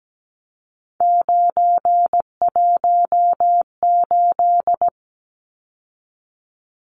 Morse code, heard as a keyed tone, about 17 wpm, 700 Hz. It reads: 918
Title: Numbers - 3-Digit 17wpm